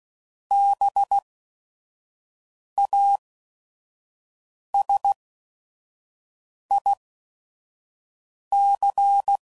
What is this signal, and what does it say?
Morse code, heard as a keyed tone, about 16 wpm, spaced as for 5 wpm, 800 Hz. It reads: BASIC
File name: CW Code Groups @5WPM / Technical Subjects / SwPwrSup